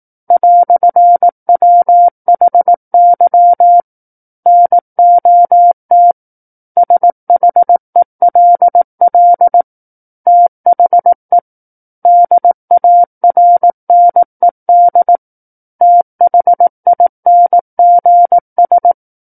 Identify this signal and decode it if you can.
Morse code, heard as a keyed tone, about 18 wpm, 700 Hz. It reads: "WHY NOT SHELL THE DARNED THINGS
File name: 18 WPM